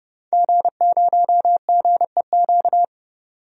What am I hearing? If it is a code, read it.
Morse code, heard as a keyed tone, about 31 wpm, 700 Hz. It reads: G0GEQ
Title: DXCC-100 Call Signs - 5 Characters - 30wpm